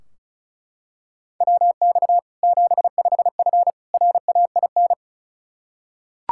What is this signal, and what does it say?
Morse code, heard as a keyed tone, about 36 wpm, 700 Hz. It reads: WX 75F RAIN
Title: Practice QSO - Exchange 2 - Info - 35wpm